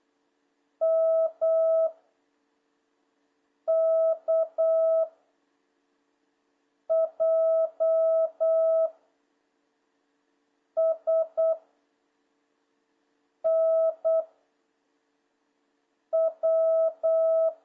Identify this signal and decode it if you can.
Morse code, heard as a keyed tone, about 8 wpm, 660 Hz. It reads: M K J S N W
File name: jh8cbh / 2026cw